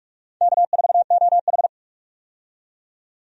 Morse code, heard as a keyed tone, about 46 wpm, 700 Hz. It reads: K4OH
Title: USA Call Signs - 4 Characters - 45wpm